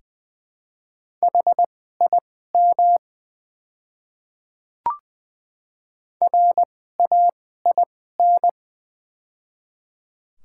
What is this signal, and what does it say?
Morse code, heard as a keyed tone, about 20 wpm, spaced as for 14 wpm, 700 Hz. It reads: HIM RAIN
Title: 2x Character Spacing - Top 300 Words - 20wpm